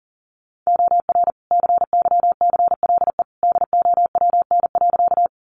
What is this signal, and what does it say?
Morse code, heard as a keyed tone, about 40 wpm, 700 Hz. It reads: OR CYCLE DOWN.
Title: Sentences from Top 900 Words 40wpm